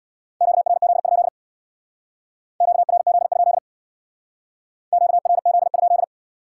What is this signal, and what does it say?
Morse code, heard as a keyed tone, about 65 wpm, 700 Hz. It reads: QRZ? QRZ? QRZ?